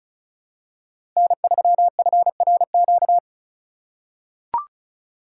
Morse code, heard as a keyed tone, about 34 wpm, 700 Hz. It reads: N3FRQ
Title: USA Call Signs - 5 Characters - 2x Character Spacing - 35wpm